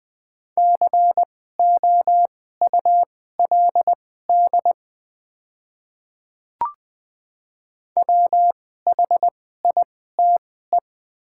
Morse code, heard as a keyed tone, about 20 wpm, spaced as for 14 wpm, 700 Hz. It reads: COULD WHITE
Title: 2x Character Spacing - Top 300 Words - 20wpm